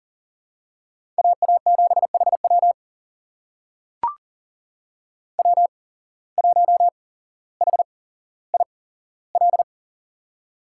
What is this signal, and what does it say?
Morse code, heard as a keyed tone, about 40 wpm, spaced as for 17 wpm, 700 Hz. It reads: AA7HW W1HIL
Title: USA Call Signs - 5 Characters - 8x Character Spacing - 40wpm